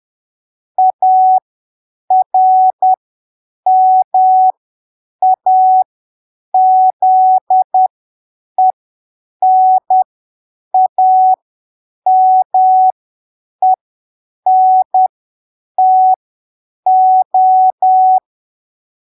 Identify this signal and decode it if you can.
Morse code, heard as a keyed tone, about 10 wpm, spaced as for 7 wpm, 750 Hz. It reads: ARMAZENAMENTO
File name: arquivos / texto / texto 09 10ppm